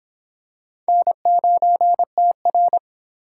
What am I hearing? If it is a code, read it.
Morse code, heard as a keyed tone, about 25 wpm, 700 Hz. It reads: N9TR